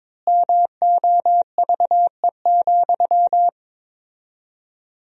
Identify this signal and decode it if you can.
Morse code, heard as a keyed tone, about 22 wpm, 700 Hz. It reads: MOVE,
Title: 4x Word-Spacing - Sentences from Top 300 Words - 22wpm